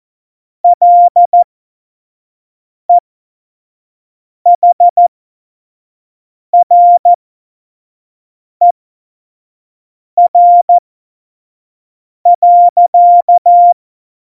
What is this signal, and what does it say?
Morse code, heard as a keyed tone, about 14 wpm, spaced as for 5 wpm, 700 Hz. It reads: LEHRER.
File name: Morsekurs / Witze / Witz 14. Mai 2019, 25 F70